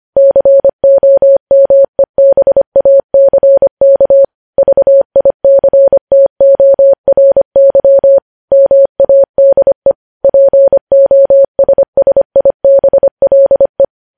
Morse code, heard as a keyed tone, about 25 wpm, 550 Hz. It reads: COMEBACK VICTORY MADE POSSIBLE
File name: podcast / News headlines-1x-25x25